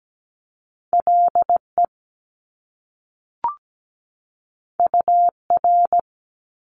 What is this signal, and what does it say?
Morse code, heard as a keyed tone, about 17 wpm, 700 Hz. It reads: LE UR
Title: Top Two Letter Combinations in Words 17wpm